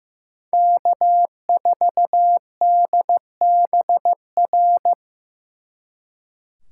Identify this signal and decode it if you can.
Morse code, heard as a keyed tone, about 18 wpm, 700 Hz. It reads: K4DBR
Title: Speed Racing - USA 1x3 Call Signs 15wpm